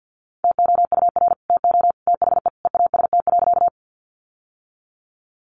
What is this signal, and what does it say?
Morse code, heard as a keyed tone, about 50 wpm, 700 Hz. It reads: TOUR TO THE EAST.
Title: Sentences from Top 1400 Words 50wpm